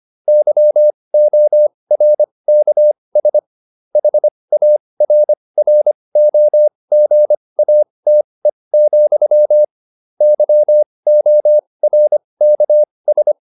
Morse code, heard as a keyed tone, about 25 wpm, spaced as for 20 wpm, 600 Hz. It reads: YORKS HARROGATE, YORKS